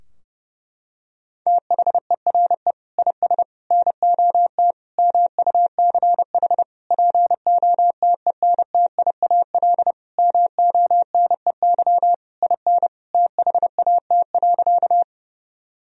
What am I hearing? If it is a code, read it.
Morse code, heard as a keyed tone, about 32 wpm, 700 Hz. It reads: THERE IS NOT MUCH POTENTIAL MONEY IN THAT.
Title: Sentences from Top 1000 Words - Review Part 4 - 30wpm